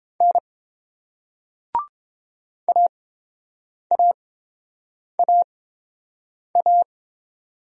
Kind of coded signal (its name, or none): Morse code